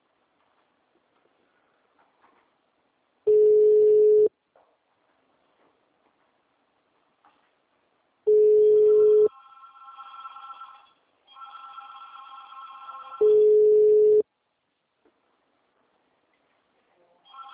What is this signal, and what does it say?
Ringback tone (Europe, 425 Hz).